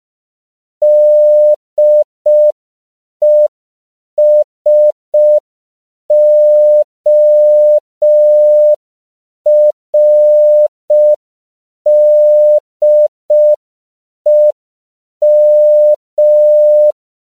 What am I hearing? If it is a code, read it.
Morse code, heard as a keyed tone, about 5 wpm, 600 Hz. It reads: DESORDEM
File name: CW / 5PPM